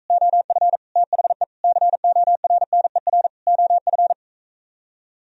Morse code, heard as a keyed tone, about 45 wpm, 700 Hz. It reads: OF THE CORNER OF